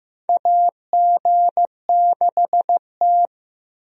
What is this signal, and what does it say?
Morse code, heard as a keyed tone, about 15 wpm, 700 Hz. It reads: AG6T
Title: USA Call Signs - 4 Characters - Repeated 2x - 15wpm